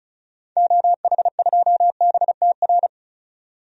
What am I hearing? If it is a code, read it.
Morse code, heard as a keyed tone, about 34 wpm, 700 Hz. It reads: OH2BTR
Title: DXCC-100 Call Signs - 6 Characters - 35wpm